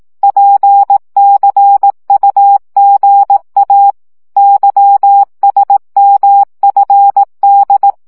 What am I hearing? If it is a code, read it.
Morse code, heard as a keyed tone, about 18 wpm, 800 Hz. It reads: PCUGA YSMFD
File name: morze vizsga